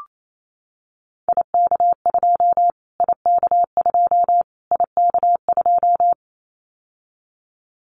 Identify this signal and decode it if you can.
Morse code, heard as a keyed tone, about 28 wpm, 700 Hz. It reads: IK2 IK2 IK2
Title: Callsign Prefixes for Top 25 DXCC - Repeated 3x - 28wpm